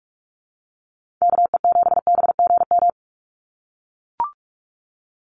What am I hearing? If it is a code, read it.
Morse code, heard as a keyed tone, about 44 wpm, 700 Hz. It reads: KE7BGM
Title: NA SOTA Activator Callsigns 45wpm